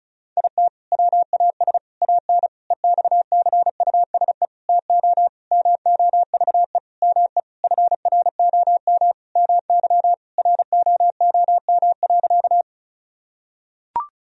Morse code, heard as a keyed tone, about 36 wpm, 700 Hz. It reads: IT WAS AN EXCUSE TO MOVE ME FROM MY ROOM.
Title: Sentences from Top 1000 Words - Review Part 1 - 35wpm